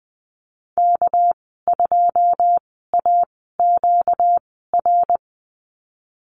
Morse code, heard as a keyed tone, about 20 wpm, spaced as for 14 wpm, 700 Hz. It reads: K2AQR
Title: 238 Lesson 38g - Letter q - Call Signs 20wpm